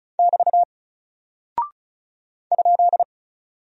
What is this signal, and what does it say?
Morse code, heard as a keyed tone, about 36 wpm, 700 Hz. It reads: <BT> ?
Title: Single Character (Letter:Number:Punctuation:Prosign) Alt 35wpm